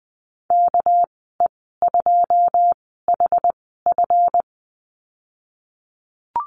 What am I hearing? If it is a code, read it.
Morse code, heard as a keyed tone, about 20 wpm, spaced as for 15 wpm, 700 Hz. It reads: KE2HF
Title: 231 Lesson 37g - Letter x - Call Signs 20wpm